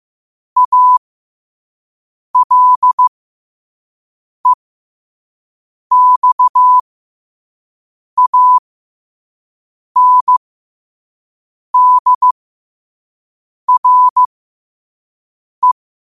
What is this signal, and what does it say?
Morse code, heard as a keyed tone, about 15 wpm, spaced as for 5 wpm, 1000 Hz. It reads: ALEXANDRE
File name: qotdfiles / 5wpm